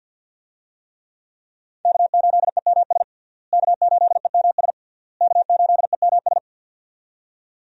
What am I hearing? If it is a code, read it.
Morse code, heard as a keyed tone, about 52 wpm, 700 Hz. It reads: K8EMS K8EMS K8EMS
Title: USA Call Signs - 5 Characters - Repeated 3x - 50wpm